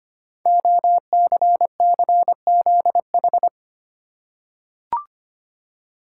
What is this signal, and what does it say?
Morse code, heard as a keyed tone, about 25 wpm, 700 Hz. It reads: OCCZH
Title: Code Groups - 5 Letters - 25wpm